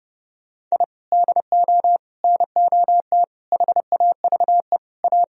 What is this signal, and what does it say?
Morse code, heard as a keyed tone, about 31 wpm, 700 Hz. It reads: I DO NOT HAVE A
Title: Speed-Racing - Sets of 5 Words 25wpm